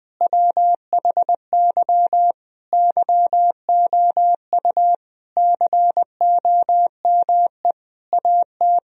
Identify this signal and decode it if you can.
Morse code, heard as a keyed tone, about 20 wpm, 700 Hz. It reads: WHY YOU COME AT
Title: Sentences from Top 1000 Words - Review Part 2 - 20wpm